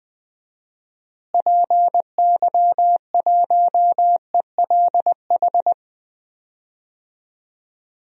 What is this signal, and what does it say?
Morse code, heard as a keyed tone, about 20 wpm, 700 Hz. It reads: PY1ELH
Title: DXCC-100 Call Signs - 6 Characters - Repeated 3x - 20wpm